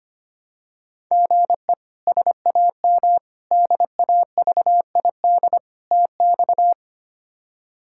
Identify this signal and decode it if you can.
Morse code, heard as a keyed tone, about 25 wpm, 700 Hz. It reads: GE SAM DAVID TX
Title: K1USN SST QSO Practice - 25wpm